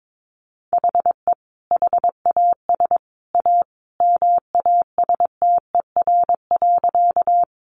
Morse code, heard as a keyed tone, about 22 wpm, 700 Hz. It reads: HE HAS A MASTER.
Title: Sentences from Top 1300 Words 22wpm